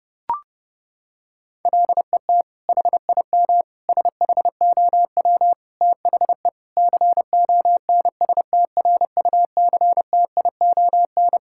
Morse code, heard as a keyed tone, about 30 wpm, 700 Hz. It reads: LET HIM SHOW THE CONSTRUCTION
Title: Sentences from Top 1600 Words 30wpm